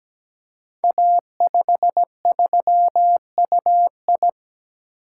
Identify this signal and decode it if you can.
Morse code, heard as a keyed tone, about 17 wpm, 700 Hz. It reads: A53UI
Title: Code Groups - 5 Letter:Numbers - 17wpm